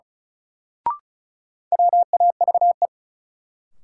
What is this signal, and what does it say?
Morse code, heard as a keyed tone, about 36 wpm, 700 Hz. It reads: WAVE
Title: Sentences from Top 1400 Words - New Word Drill Alt - 35wpm